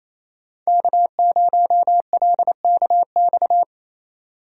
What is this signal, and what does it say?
Morse code, heard as a keyed tone, about 28 wpm, 700 Hz. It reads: K0LKX
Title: USA Call Signs - 5 Characters - 4x Character Spacing - 28wpm